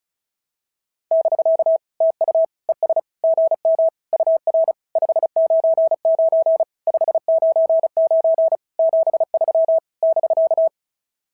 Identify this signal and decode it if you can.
Morse code, heard as a keyed tone, about 36 wpm, 690 Hz. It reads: <BK> TU ES GM UR 599 599 73 <BK>
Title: SOTA QSO Practice 35wpm